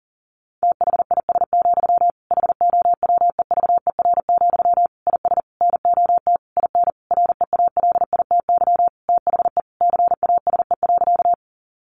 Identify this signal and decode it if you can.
Morse code, heard as a keyed tone, about 40 wpm, 700 Hz. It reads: THIS, HOWEVER, IS NOT IN REALITY THE CASE.
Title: Sentences from Top 1200 Words 40wpm